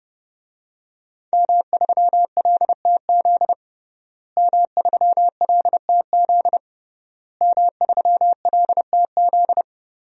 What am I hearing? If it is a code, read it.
Morse code, heard as a keyed tone, about 30 wpm, 700 Hz. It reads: M3LTZ M3LTZ M3LTZ
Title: DXCC-100 Call Signs - 5 Characters - Repeated 3x - 30wpm